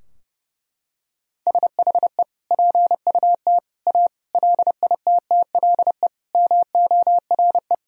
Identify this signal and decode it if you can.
Morse code, heard as a keyed tone, about 32 wpm, 700 Hz. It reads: SHE PUT A LITTLE MORE
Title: Sentences from Top 1000 Words - Review Part 2 - 30wpm